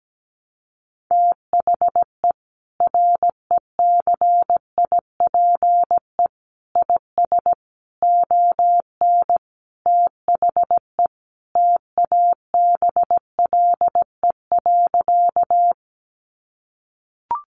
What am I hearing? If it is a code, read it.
Morse code, heard as a keyed tone, about 17 wpm, 700 Hz. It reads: THE RECIPE IS ON THE TABLE.
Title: Sentences from Top 1200 Words 17wpm